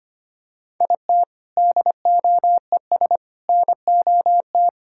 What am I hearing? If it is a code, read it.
Morse code, heard as a keyed tone, about 25 wpm, 700 Hz. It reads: IT DOES NOT
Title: Sentences from Top 1000 Words - Review Part 6 - 25wpm